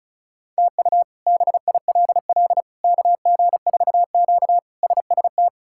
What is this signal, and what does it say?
Morse code, heard as a keyed tone, about 36 wpm, 700 Hz. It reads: TU BILL KG4Q SST
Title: K1USN SST QSO Practice - 35wpm